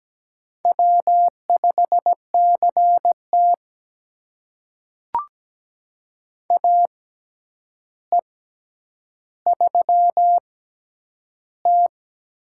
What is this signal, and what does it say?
Morse code, heard as a keyed tone, about 17 wpm, spaced as for 9 wpm, 700 Hz. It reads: W5CT AE3T
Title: USA Call Signs - 4 Characters - 6x Character Spacing - 17wpm